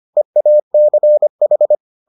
Morse code, heard as a keyed tone, about 25 wpm, 600 Hz. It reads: EACH